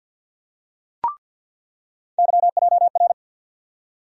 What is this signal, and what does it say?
Morse code, heard as a keyed tone, about 50 wpm, 700 Hz. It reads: YJR
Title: USA Call Sign Postfixes - 3 Letters 50wpm